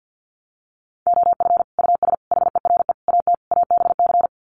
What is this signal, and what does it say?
Morse code, heard as a keyed tone, about 48 wpm, 700 Hz. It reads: OF US HERE AT ABC
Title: Sets of 5 Words 50wpm